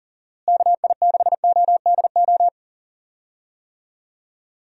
Morse code, heard as a keyed tone, about 39 wpm, 700 Hz. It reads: KI6ODO
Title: USA Call Signs - 6 Characters - Repeated 3x - 40wpm